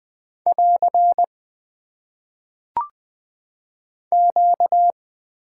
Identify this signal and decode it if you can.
Morse code, heard as a keyed tone, about 20 wpm, 700 Hz. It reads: <AR> Q
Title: Top 90 QSO Elements - 6x Character Spacing - 20wpm